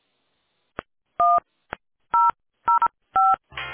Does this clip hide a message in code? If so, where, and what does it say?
Yes: DTMF from 1.2 to 3.4 s, heard as telephone touch tones, tones of about 210 ms. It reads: 1005